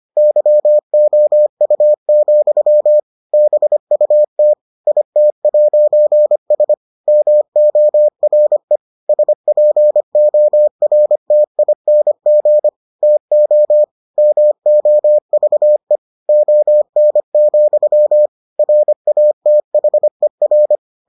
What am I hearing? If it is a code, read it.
Morse code, heard as a keyed tone, about 25 wpm, 600 Hz. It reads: YOU, BUT IT'S MORE SPORTING TO MOVE ON, RATHER